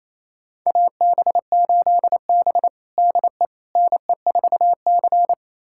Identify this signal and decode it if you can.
Morse code, heard as a keyed tone, about 27 wpm, 700 Hz. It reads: AB8B DE NE4C
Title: USA Callsign DE Callsign - 4 Characters - 28wpm